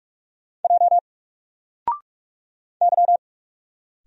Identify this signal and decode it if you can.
Morse code, heard as a keyed tone, about 44 wpm, 700 Hz. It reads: J Y
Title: Single Letter-Number 45wpm